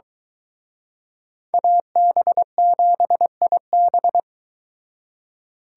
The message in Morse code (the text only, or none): AB7IB